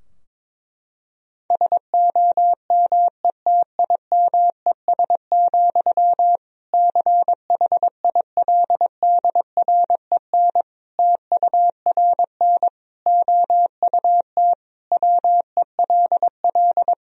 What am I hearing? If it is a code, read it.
Morse code, heard as a keyed tone, about 21 wpm, 700 Hz. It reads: SOMETIMES, CHILDREN TURN OUT WELL
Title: Sentences from Top 1000 Words - Review Part 3 - 22wpm